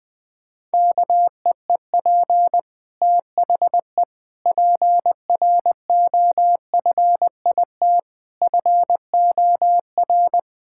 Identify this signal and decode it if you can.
Morse code, heard as a keyed tone, about 20 wpm, 700 Hz. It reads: KEEP THE PROFIT FOR
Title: Sentences from Top 1000 Words - Review Part 5 - 20wpm